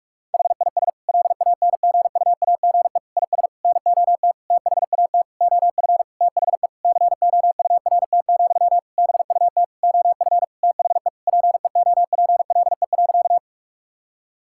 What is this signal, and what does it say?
Morse code, heard as a keyed tone, about 46 wpm, 700 Hz. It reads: HIS LANGUAGE IS NOT THAT OF THE COURT, BUT OF THE PEOPLE.
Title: Sentences from Top 1600 Words 45wpm